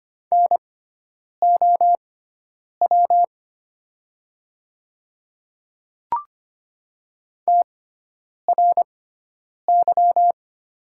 Morse code, heard as a keyed tone, about 25 wpm, spaced as for 9 wpm, 700 Hz. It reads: NOW TRY
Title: 6x Character Spacing - Top 300 Words - 25wpm